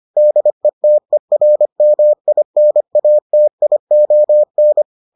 Morse code, heard as a keyed tone, about 25 wpm, 600 Hz. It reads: DETERMINATION